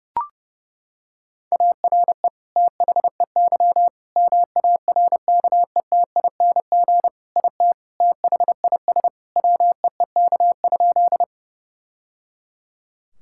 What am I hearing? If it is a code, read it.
Morse code, heard as a keyed tone, about 30 wpm, 700 Hz. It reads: ARE THEY MARKETING IT THIS WEEK?
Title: Sentences from Top 1000 Words - Review Part 5 - 30wpm